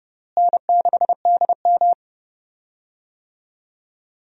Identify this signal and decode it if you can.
Morse code, heard as a keyed tone, about 30 wpm, 700 Hz. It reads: N6DM
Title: USA Call Signs - 4 Characters - Repeated 3x - 30wpm